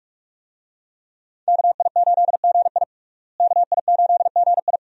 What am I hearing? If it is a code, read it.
Morse code, heard as a keyed tone, about 43 wpm, 700 Hz. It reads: KI8GI KI8GI
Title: USA Call Signs - 5 Characters - Repeated 2x - 45wpm